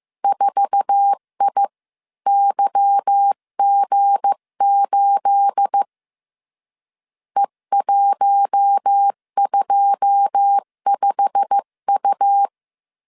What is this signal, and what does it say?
Morse code, heard as a keyed tone, about 15 wpm, 800 Hz. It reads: 4IYG8 E125U
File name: CW / DET / 1 9 1500C S B 14wpm 3302